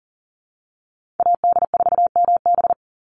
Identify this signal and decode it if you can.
Morse code, heard as a keyed tone, about 40 wpm, 700 Hz. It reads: AD4MB
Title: USA Call Signs 40wpm